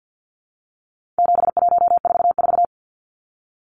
Morse code, heard as a keyed tone, about 50 wpm, 700 Hz. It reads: 7144
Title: Numbers - 4-Digit 50wpm